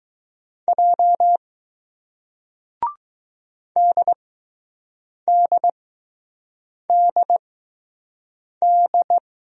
Morse code, heard as a keyed tone, about 21 wpm, 700 Hz. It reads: J D D D D